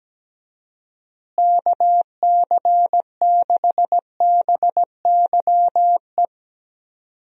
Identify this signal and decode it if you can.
Morse code, heard as a keyed tone, about 17 wpm, 700 Hz. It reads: KC6BYE
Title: USA Call Signs - 6 Characters - Repeated 3x - 17wpm